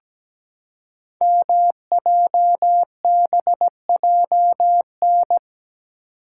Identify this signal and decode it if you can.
Morse code, heard as a keyed tone, about 17 wpm, 700 Hz. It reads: MJBJN
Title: Code Groups - 5 Letters - 17wpm